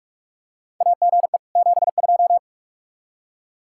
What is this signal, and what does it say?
Morse code, heard as a keyed tone, about 44 wpm, 700 Hz. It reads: AGE 72
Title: Practice QSO - Exchange 3 - More Info - 45wpm